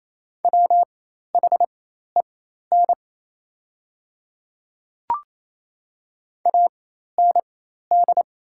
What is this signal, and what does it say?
Morse code, heard as a keyed tone, about 28 wpm, spaced as for 13 wpm, 690 Hz. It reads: WHEN AND
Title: 4x Character Spacing - Top 100 Words - 28wpm